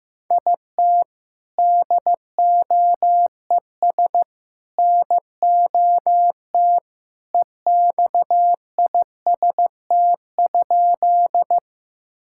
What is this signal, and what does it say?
Morse code, heard as a keyed tone, about 15 wpm, 700 Hz. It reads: IT DOES NOT EXIST?
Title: Sentences from Top 1000 Words - Review Part 5 - 15wpm